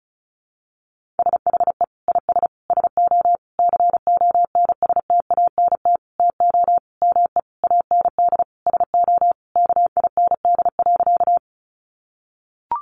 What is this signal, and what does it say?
Morse code, heard as a keyed tone, about 34 wpm, 700 Hz. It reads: SHE IS SO CONSTANT TO ME AND SO KIND.
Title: Sentences from Top 1500 Words 35wpm